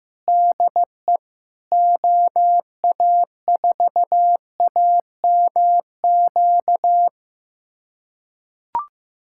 Morse code, heard as a keyed tone, about 15 wpm, 700 Hz. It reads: DE OA4AMQ
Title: USA Callsign DE Callsign - 6 Characters - 15wpm